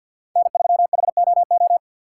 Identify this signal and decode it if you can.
Morse code, heard as a keyed tone, about 53 wpm, 700 Hz. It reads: N3HOO
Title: USA Call Signs - 5 Characters - 4x Character Spacing - 50wpm